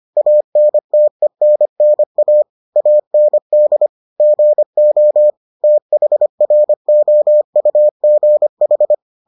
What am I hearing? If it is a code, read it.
Morse code, heard as a keyed tone, about 25 wpm, 600 Hz. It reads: ANTENNA AND GO THROUGH